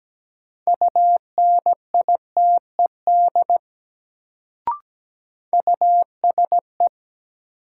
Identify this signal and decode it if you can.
Morse code, heard as a keyed tone, about 17 wpm, 700 Hz. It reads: UNITED USE
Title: Sets of 2 Words - New Word Drill - 17wpm